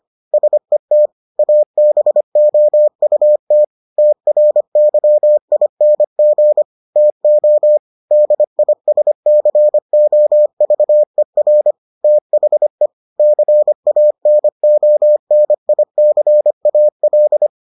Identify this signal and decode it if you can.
Morse code, heard as a keyed tone, about 25 wpm, 600 Hz. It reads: SET ABOUT TRYING TO DISCOVER THE CANONICAL